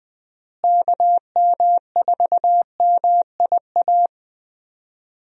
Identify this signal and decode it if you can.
Morse code, heard as a keyed tone, about 20 wpm, 700 Hz. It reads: KM4MIA
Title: USA Call Signs - 6 Characters - Repeated 3x - 20wpm